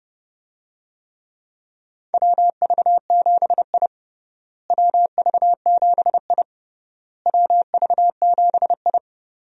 Morse code, heard as a keyed tone, about 32 wpm, 700 Hz. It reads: WV7I WV7I WV7I
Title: USA Call Signs - 4 Characters - Repeated 3x - 30wpm